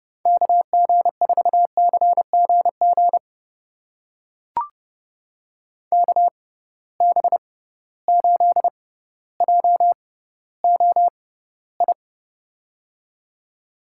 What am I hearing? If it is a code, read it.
Morse code, heard as a keyed tone, about 30 wpm, spaced as for 15 wpm, 700 Hz. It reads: KG4CGG KB8JOI
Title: USA Call Signs - 6x Character Spacing - 30wpm